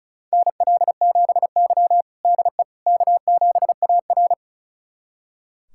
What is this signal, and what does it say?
Morse code, heard as a keyed tone, about 36 wpm, 700 Hz. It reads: NL7Y DE K7AR
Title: USA Callsign DE Callsign - 4 Characters - 35wpm